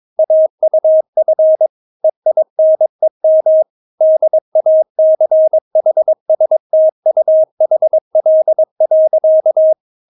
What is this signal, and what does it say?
Morse code, heard as a keyed tone, about 22 wpm, 630 Hz. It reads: AUF EINEM DACHSTUHL.
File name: Witz3 22wpm